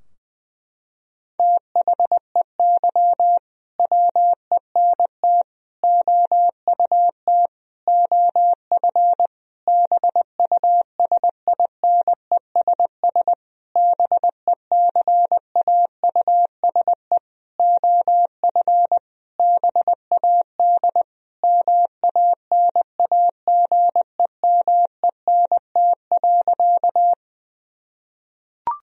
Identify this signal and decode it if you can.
Morse code, heard as a keyed tone, about 20 wpm, 700 Hz. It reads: THEY WENT OUT OF BUSINESS BECAUSE OF BAD MANAGEMENT.
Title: Sentences from Top 1000 Words - Review Part 4 - 20wpm